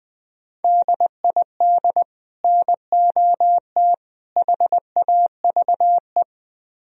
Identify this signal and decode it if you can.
Morse code, heard as a keyed tone, about 20 wpm, 700 Hz. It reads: DID NOT HAVE
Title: Speed-Racing - Sets of 3 Words 17wpm